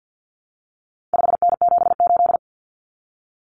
Morse code, heard as a keyed tone, about 50 wpm, 700 Hz. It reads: 5N78
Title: Common Cut Numbers - 4-Digit 50wpm